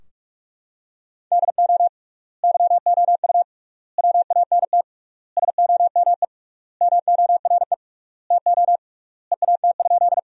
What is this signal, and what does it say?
Morse code, heard as a keyed tone, about 42 wpm, 700 Hz. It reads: DO YOU WANT SOME MORE TO EAT?
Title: 3x Word-Spacing - Sentences from Top 300 Words - 45wpm